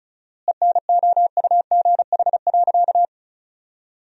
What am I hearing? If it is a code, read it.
Morse code, heard as a keyed tone, about 34 wpm, 700 Hz. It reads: ENOUGH.
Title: 4x Word-Spacing - Sentences from Top 400 Words - 35wpm